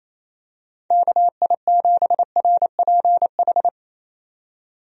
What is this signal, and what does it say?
Morse code, heard as a keyed tone, about 27 wpm, 700 Hz. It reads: KI7RPH
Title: USA Call Signs - 6 Characters - 28wpm